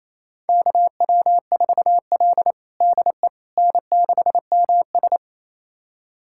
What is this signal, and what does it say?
Morse code, heard as a keyed tone, about 28 wpm, 700 Hz. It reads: KW4L DE N6MS
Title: USA Callsign DE Callsign - 4 Characters - 28wpm